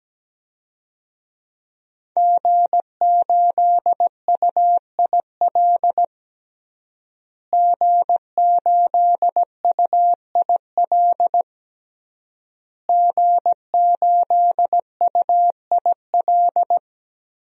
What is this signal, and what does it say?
Morse code, heard as a keyed tone, about 17 wpm, 700 Hz. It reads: G8UIL G8UIL G8UIL
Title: DXCC-100 Call Signs - 5 Characters - Repeated 3x - 17wpm